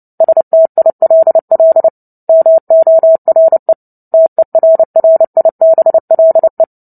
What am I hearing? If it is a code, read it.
Morse code, heard as a keyed tone, about 30 wpm, 650 Hz. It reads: STILL MORE TERRIBLE
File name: Book Two / 03-Chapter 3 - 29 WPM